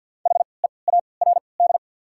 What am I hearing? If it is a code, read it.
Morse code, heard as a keyed tone, about 53 wpm, spaced as for 29 wpm, 700 Hz. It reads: HEARD
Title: 3x Character Spacing - Top 500 Words - 50wpm